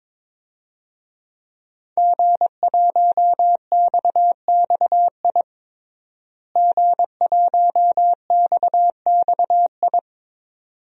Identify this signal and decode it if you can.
Morse code, heard as a keyed tone, about 21 wpm, 700 Hz. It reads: G1XXI G1XXI
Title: DXCC-100 Call Signs - 5 Characters - Repeated 2x - 22wpm